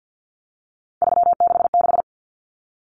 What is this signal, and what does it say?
Morse code, heard as a keyed tone, about 50 wpm, 700 Hz. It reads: 366
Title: Numbers - 3-Digit 50wpm